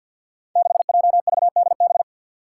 Morse code, heard as a keyed tone, about 48 wpm, 700 Hz. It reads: 6JVDB
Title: Five Challenging Letter-Number Combinations 50wpm